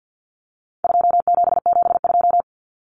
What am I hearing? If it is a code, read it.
Morse code, heard as a keyed tone, about 50 wpm, 700 Hz. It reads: 2772
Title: Common Cut Numbers - 4-Digit 50wpm